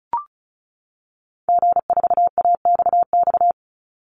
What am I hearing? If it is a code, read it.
Morse code, heard as a keyed tone, about 34 wpm, 700 Hz. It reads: G4AXX
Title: Call Signs - 35wpm